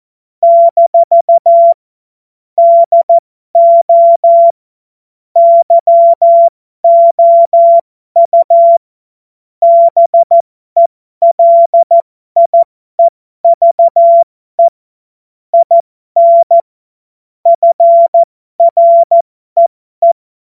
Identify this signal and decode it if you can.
Morse code, heard as a keyed tone, about 14 wpm, spaced as for 12 wpm, 680 Hz. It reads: - DO YOU BELIEVE IN FREE
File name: Morsekurs / Witze